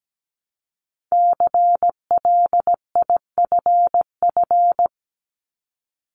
Morse code, heard as a keyed tone, about 17 wpm, 700 Hz. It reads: CLIFF
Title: Multi-letter Phonemes Repeated 3x 17wpm